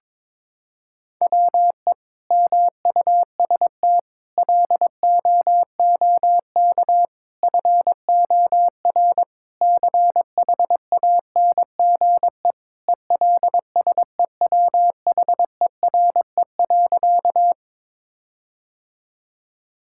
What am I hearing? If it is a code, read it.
Morse code, heard as a keyed tone, about 22 wpm, 700 Hz. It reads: WE MUST LOOK FOR CHANGE ELSEWHERE.
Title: Sentences from Top 1600 Words 22wpm